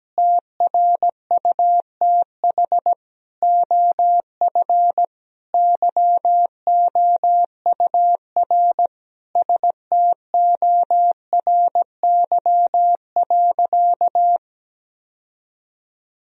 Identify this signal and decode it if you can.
Morse code, heard as a keyed tone, about 17 wpm, 700 Hz. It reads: TRUTH OF YOUR STORY.
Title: Sentences from Top 1000 Words - Review Part 3 - 17wpm